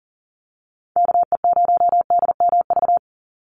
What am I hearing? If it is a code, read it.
Morse code, heard as a keyed tone, about 40 wpm, 700 Hz. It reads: KE0DMV
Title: USA Call Signs 40wpm